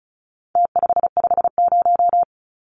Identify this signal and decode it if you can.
Morse code, heard as a keyed tone, about 34 wpm, 700 Hz. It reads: T550
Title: Numbers - Contest Style - 4-Digit 35wpm